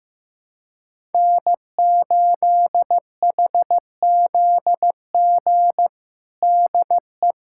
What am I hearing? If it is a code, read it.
Morse code, heard as a keyed tone, about 15 wpm, 700 Hz. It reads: N8HZG DE
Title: USA Callsign DE Callsign 15wpm